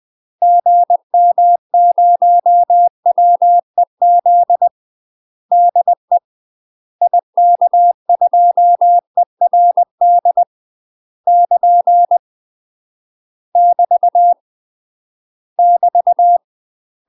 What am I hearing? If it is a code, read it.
Morse code, heard as a keyed tone, about 20 wpm, 700 Hz. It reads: GM0WEZ DE IK2ERD <KN> <BT> <BT>
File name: UK MASTER QSOs TRAINING / 20WPM / UK MASTER QSOs 0008